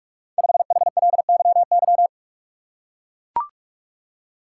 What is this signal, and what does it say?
Morse code, heard as a keyed tone, about 44 wpm, 700 Hz. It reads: 5HLYY
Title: Code Groups - 5 Letter:Numbers - 45wpm